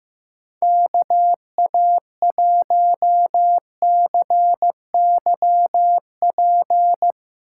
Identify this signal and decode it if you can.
Morse code, heard as a keyed tone, about 15 wpm, 700 Hz. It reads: KA1CYP
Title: USA Call Signs - 6 Characters - 2x Character Spacing - 15wpm